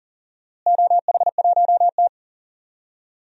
Morse code, heard as a keyed tone, about 39 wpm, 700 Hz. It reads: OH1T
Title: DXCC-100 Call Signs - 4 Characters - Repeated 3x - 40wpm